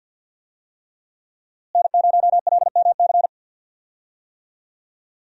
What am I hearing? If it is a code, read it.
Morse code, heard as a keyed tone, about 52 wpm, 700 Hz. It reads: N0LMC
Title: USA Call Signs - 5 Characters - Repeated 3x - 50wpm